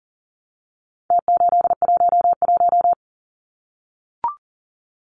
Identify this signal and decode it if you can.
Morse code, heard as a keyed tone, about 40 wpm, 700 Hz. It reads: T811
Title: Numbers - Contest Style - 4-Digit 40wpm